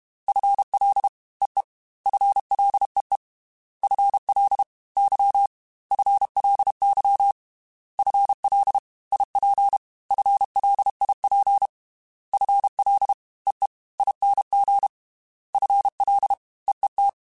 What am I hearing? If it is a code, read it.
Morse code, heard as a keyed tone, about 32 wpm, 800 Hz. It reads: FL EE FLEE FL Y FLY FL IP FLIP FL EE ING FL EET
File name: FL / FL mp3s / Fast FL @30